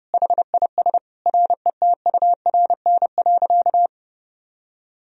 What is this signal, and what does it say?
Morse code, heard as a keyed tone, about 32 wpm, 700 Hz. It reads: HIS RETURN.
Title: Sentences from Top 1000 Words - Review Part 1 - 30wpm